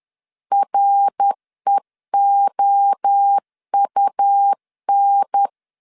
Morse code, heard as a keyed tone, about 10 wpm, 800 Hz. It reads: REOUN
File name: CW / DET / 1 3 1500C S B 10wpm 3335